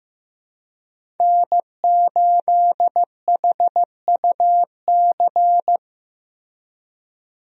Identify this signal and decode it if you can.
Morse code, heard as a keyed tone, about 15 wpm, 700 Hz. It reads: N8HUC
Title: USA Call Signs - 5 Characters - Repeated 3x - 15wpm